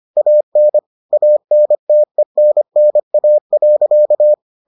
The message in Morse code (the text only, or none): AN ANTENNA.